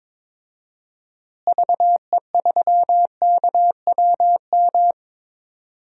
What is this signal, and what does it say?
Morse code, heard as a keyed tone, about 22 wpm, 700 Hz. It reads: VE3KWM
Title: DXCC-100 Call Signs - 6 Characters - Repeated 3x - 22wpm